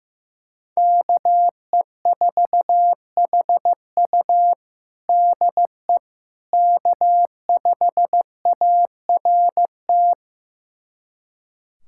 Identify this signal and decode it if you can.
Morse code, heard as a keyed tone, about 15 wpm, 700 Hz. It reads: KE4HU DE K5ART
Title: USA Callsign DE Callsign - 5 Characters - 15wpm